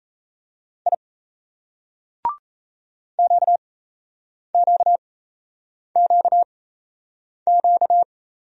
Morse code, heard as a keyed tone, about 36 wpm, spaced as for 8 wpm, 700 Hz. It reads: I QQQQ